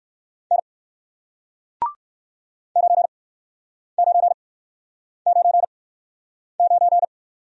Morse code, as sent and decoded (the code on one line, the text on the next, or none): -. / ----. ----. ----. ----.
N 9999